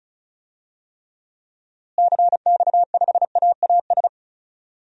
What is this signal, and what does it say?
Morse code, heard as a keyed tone, about 34 wpm, 700 Hz. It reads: CX5AAS
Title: DXCC-100 Call Signs - 6 Characters - Repeated 3x - 35wpm